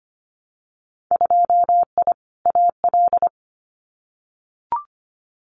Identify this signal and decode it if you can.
Morse code, heard as a keyed tone, about 25 wpm, 700 Hz. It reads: 2I AL
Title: Winter Field Day Exchanges 25wpm